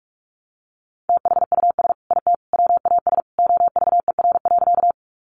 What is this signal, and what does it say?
Morse code, heard as a keyed tone, about 44 wpm, 700 Hz. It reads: THUS IT WAS OVER.
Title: Sentences from Top 800 Words 45wpm